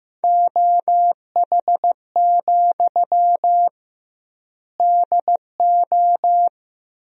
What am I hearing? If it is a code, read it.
Morse code, heard as a keyed tone, about 15 wpm, 700 Hz. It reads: OH, DO
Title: 2x Word-Spacing - Sentences from Top 500 Words - 15wpm